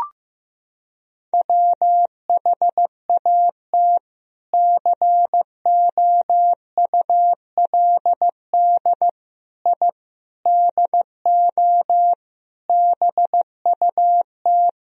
Morse code, heard as a keyed tone, about 15 wpm, 700 Hz. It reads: WHAT COULD I DO BUT
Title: Sentences from Top 1000 Words - Review Part 1 - 15wpm